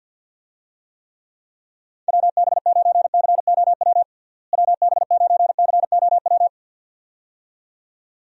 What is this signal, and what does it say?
Morse code, heard as a keyed tone, about 48 wpm, 700 Hz. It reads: WB9COW WB9COW
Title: USA Call Signs - 6 Characters - Repeated 2x - 50wpm